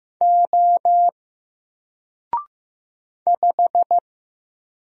Morse code, heard as a keyed tone, about 15 wpm, 700 Hz. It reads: O 5
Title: Single Letter-Number Alt 15wpm